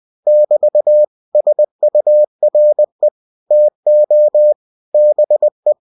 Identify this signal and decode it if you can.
Morse code, heard as a keyed tone, about 20 wpm, 600 Hz. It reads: <BT> SURE TO BE
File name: cw-books / winnie-the-pooh / winnie-20wpm-page-0085